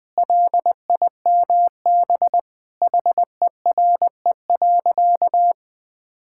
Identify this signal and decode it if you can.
Morse code, heard as a keyed tone, about 21 wpm, 700 Hz. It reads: LIMB HERE.
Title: Sentences with 8-Word Phrases - 20wpm